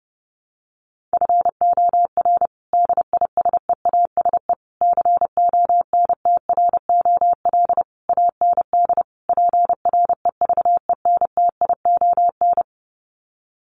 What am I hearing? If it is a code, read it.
Morse code, heard as a keyed tone, about 28 wpm, 700 Hz. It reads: FOR DISEASE CONTROL AND PREVENTION
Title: Sets of 5 Words Encore2 30wpm